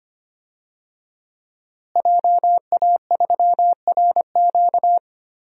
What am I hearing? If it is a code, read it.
Morse code, heard as a keyed tone, about 25 wpm, 700 Hz. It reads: JA3RQ
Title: DXCC-100 Call Signs - 5 Characters - Repeated 3x - 25wpm